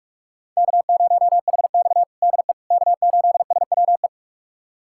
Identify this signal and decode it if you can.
Morse code, heard as a keyed tone, about 46 wpm, 700 Hz. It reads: K0HX DE K8SWE
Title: USA Callsign DE Callsign 45wpm